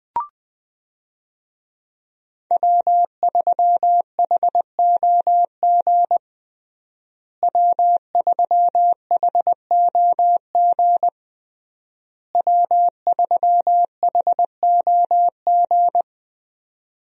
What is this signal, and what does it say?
Morse code, heard as a keyed tone, about 20 wpm, 700 Hz. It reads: W3HOG W3HOG W3HOG
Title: USA Call Signs - 5 Characters - Repeated 3x - 20wpm